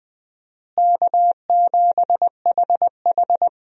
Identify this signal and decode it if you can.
Morse code, heard as a keyed tone, about 20 wpm, 700 Hz. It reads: K7HH
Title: USA Call Signs - 4 Characters - Repeated 2x - 20wpm